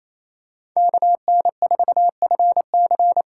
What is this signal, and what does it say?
Morse code, heard as a keyed tone, about 28 wpm, 700 Hz. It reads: KN4FC
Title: USA Call Signs - 5 Characters - 6x Character Spacing - 28wpm